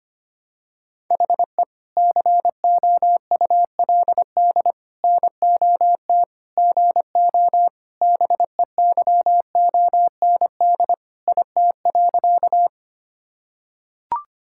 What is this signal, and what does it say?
Morse code, heard as a keyed tone, about 25 wpm, 700 Hz. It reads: HE COULD NOT GO BEYOND IT.
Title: Sentences from Top 1000 Words - Review Part 6 - 25wpm